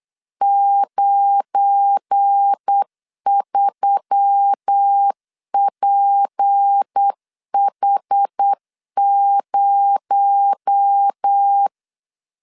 Morse code, heard as a keyed tone, about 8 wpm, 800 Hz. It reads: 93PH0
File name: LEZIONI / 8 YJ45 / 8 YJ4 8wpm 3246